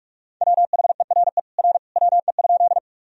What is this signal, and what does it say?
Morse code, heard as a keyed tone, about 46 wpm, 700 Hz. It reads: WHERE R WE?
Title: Sentences from Top 200 Words and Abbreviations 45wpm